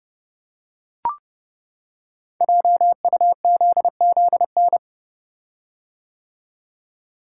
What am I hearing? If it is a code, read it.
Morse code, heard as a keyed tone, about 30 wpm, 700 Hz. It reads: JUZZN
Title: Code Groups - 5 Letters - 30wpm